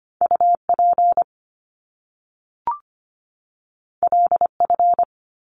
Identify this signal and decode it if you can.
Morse code, heard as a keyed tone, about 25 wpm, 700 Hz. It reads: U P L F
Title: Top Two Letter Combinations in Words 25wpm